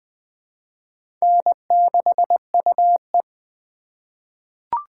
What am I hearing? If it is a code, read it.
Morse code, heard as a keyed tone, about 20 wpm, 700 Hz. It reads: N6UE
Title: USA Call Signs - 4x Character Spacing - 20wpm